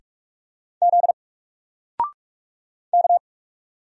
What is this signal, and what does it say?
Morse code, heard as a keyed tone, about 44 wpm, 700 Hz. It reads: Z K
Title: Single Letter-Number Alt 45wpm